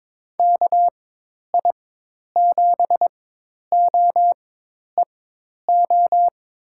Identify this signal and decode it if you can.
Morse code, heard as a keyed tone, about 22 wpm, spaced as for 10 wpm, 700 Hz. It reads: KI7OEO
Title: USA Call Signs - 6 Characters - 4x Character Spacing - 22wpm